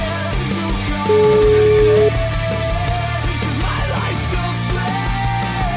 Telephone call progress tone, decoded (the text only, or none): ringback tone